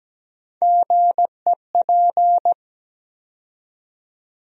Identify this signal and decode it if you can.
Morse code, heard as a keyed tone, about 17 wpm, 700 Hz. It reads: GEP
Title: USA Call Sign Postfixes - 3 Letters 17wpm